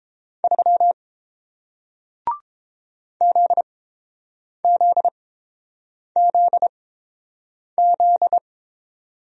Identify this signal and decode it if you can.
Morse code, heard as a keyed tone, about 30 wpm, spaced as for 7 wpm, 700 Hz. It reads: 3 ZZZZ